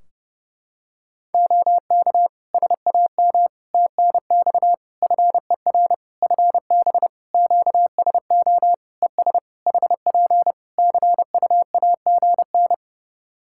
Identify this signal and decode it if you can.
Morse code, heard as a keyed tone, about 30 wpm, 700 Hz. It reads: OK SAM TNX FER FB QSO ES HP CUAGN
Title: Practice QSO - Exchange 4 - Ending - 30wpm